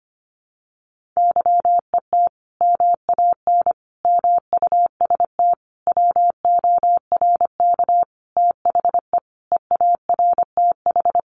Morse code, heard as a keyed tone, about 25 wpm, 700 Hz. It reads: YET MAN MUST WORK THE EARTH
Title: Sentences from Top 1500 Words 25wpm